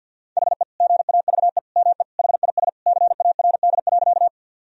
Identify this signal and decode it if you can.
Morse code, heard as a keyed tone, about 50 wpm, 700 Hz. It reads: HE GAVE ME HIS CARD.
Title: Sentences from Top 1000 Words - Review Part 4 - 50wpm